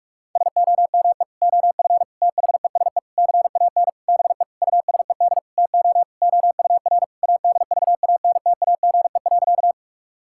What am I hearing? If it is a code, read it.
Morse code, heard as a keyed tone, about 46 wpm, 700 Hz. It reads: SOME OF THESE CAN BE USED TO OUR ADVANTAGE.
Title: Sentences from Top 1000 Words - Review Part 6 - 45wpm